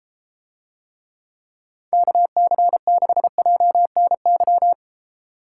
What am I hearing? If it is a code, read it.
Morse code, heard as a keyed tone, about 33 wpm, 700 Hz. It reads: KC6JNY